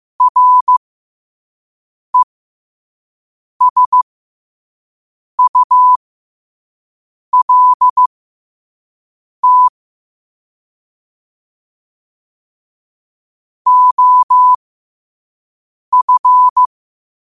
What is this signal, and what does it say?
Morse code, heard as a keyed tone, about 15 wpm, spaced as for 5 wpm, 1000 Hz. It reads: RESULT OF
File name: qotdfiles / 5wpm